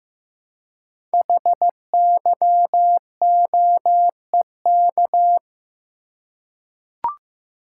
Morse code, heard as a keyed tone, about 15 wpm, 700 Hz. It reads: HYOEK
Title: Code Groups - 5 Letters - 15wpm